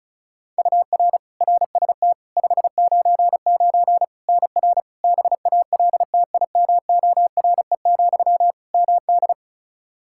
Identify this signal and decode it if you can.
Morse code, heard as a keyed tone, about 34 wpm, 700 Hz. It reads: UR RST 599 NR BALTIMORE, MD
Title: Practice QSO - Exchange 1 - Beginning - 35wpm